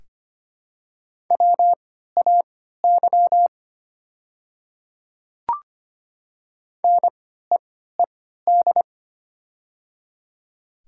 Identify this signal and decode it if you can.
Morse code, heard as a keyed tone, about 25 wpm, spaced as for 14 wpm, 700 Hz. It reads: WAY NEED
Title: 3x Character Spacing - Top 200 Words - 25wpm